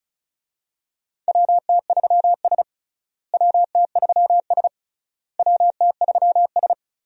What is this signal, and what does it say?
Morse code, heard as a keyed tone, about 36 wpm, 700 Hz. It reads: WT3S WT3S WT3S
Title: USA Call Signs - 4 Characters - Repeated 3x - 35wpm